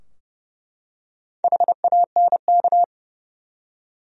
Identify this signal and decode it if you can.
Morse code, heard as a keyed tone, about 30 wpm, 700 Hz. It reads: HANK
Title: USA Common Names - Max 4 Letters - 30wpm